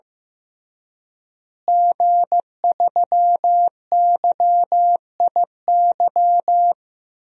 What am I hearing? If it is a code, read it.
Morse code, heard as a keyed tone, about 15 wpm, 700 Hz. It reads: G3YIY
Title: DXCC-100 Call Signs - 5 Characters - Repeated 2x - 15wpm